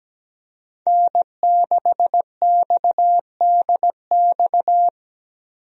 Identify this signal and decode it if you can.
Morse code, heard as a keyed tone, about 17 wpm, 700 Hz. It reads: N6XDX
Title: USA Call Signs - 5 Characters - 17wpm